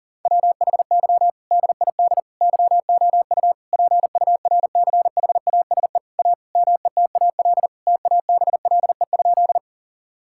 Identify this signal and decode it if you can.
Morse code, heard as a keyed tone, about 38 wpm, 700 Hz. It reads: WHY DID YOU PURCHASE A METAL TABLE?
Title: Sentences from Top 1000 Words - Review Part 6 - 40wpm